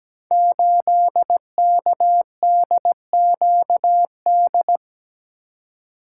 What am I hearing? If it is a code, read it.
Morse code, heard as a keyed tone, about 17 wpm, 700 Hz. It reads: 8KDQD
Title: Code Groups - 5 Letter:Numbers - 17wpm